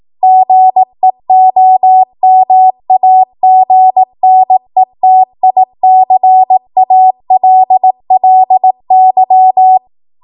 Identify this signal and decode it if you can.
Morse code, heard as a keyed tone, about 18 wpm, 750 Hz. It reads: GEOMAGNETICALLY
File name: Morse / 18WPM